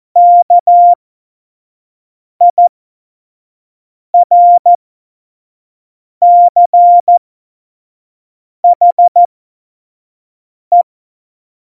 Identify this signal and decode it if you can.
Morse code, heard as a keyed tone, about 14 wpm, spaced as for 5 wpm, 700 Hz. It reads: KIRCHE
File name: Morsekurs / Witze / Witz 22. Dezember 2020, 25 F70